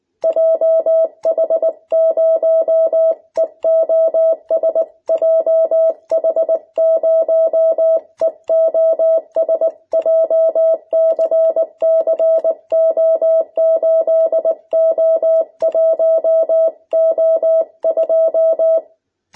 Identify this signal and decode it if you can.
Morse code, heard as a keyed tone, about 19 wpm, 650 Hz. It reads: JH0EOS JH0EOS JCCO8O1O2
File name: jh8cbh / 2021cw